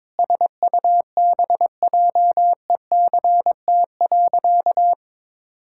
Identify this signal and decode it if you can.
Morse code, heard as a keyed tone, about 22 wpm, 700 Hz. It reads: SUBJECT.
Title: Sentences from Top 1000 Words - Review Part 6 - 22wpm